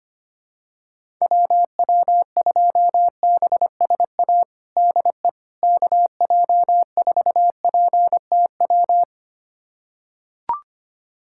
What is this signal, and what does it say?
Morse code, heard as a keyed tone, about 25 wpm, 700 Hz. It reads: WW2BSA DE KJ4PTW
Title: USA Callsign DE Callsign - 6 Characters - 25wpm